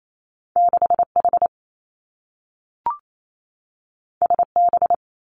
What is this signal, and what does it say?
Morse code, heard as a keyed tone, about 28 wpm, 700 Hz. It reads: 6 H S B
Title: Two Challenging Letter-Number Combinations 28wpm